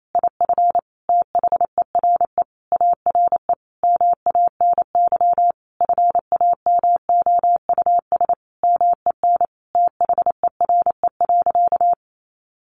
Morse code, heard as a keyed tone, about 28 wpm, 700 Hz. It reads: IF THERE ARE MANY FAMOUS MEN THERE.
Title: Sentences from Top 1500 Words 28wpm